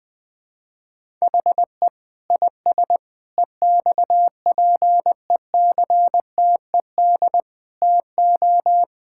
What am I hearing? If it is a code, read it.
Morse code, heard as a keyed tone, about 20 wpm, 700 Hz. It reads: HE IS EXPECTED TO
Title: Sentences with 7-Word Phrases - 20wpm